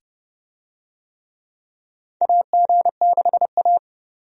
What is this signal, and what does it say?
Morse code, heard as a keyed tone, about 30 wpm, 700 Hz. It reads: AG6A
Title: USA Call Signs - 4 Characters - 30wpm